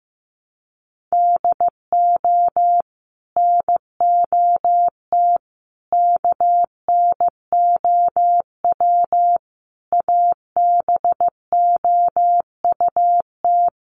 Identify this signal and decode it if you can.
Morse code, heard as a keyed tone, about 15 wpm, 700 Hz. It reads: DO NOT KNOW ABOUT
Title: Sets of 5 Words Encore2 15wpm